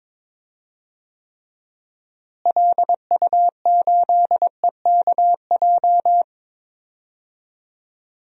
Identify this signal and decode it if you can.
Morse code, heard as a keyed tone, about 22 wpm, 700 Hz. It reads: LU8EKJ
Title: DXCC-100 Call Signs - 6 Characters - Repeated 3x - 22wpm